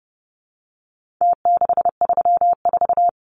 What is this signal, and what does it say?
Morse code, heard as a keyed tone, about 29 wpm, 700 Hz. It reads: T634
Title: Numbers - Contest Style - 4-Digit 30wpm